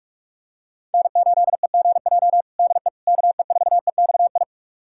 Morse code, heard as a keyed tone, about 42 wpm, 700 Hz. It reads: N8EGJ DE KE4EXI
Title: USA Callsign DE Callsign 45wpm